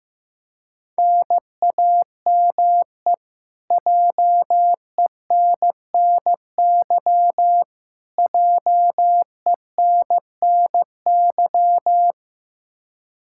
Morse code, heard as a keyed tone, about 15 wpm, 700 Hz. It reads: NAME JENNY JENNY
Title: Practice QSO - Exchange 1 - Beginning - 15wpm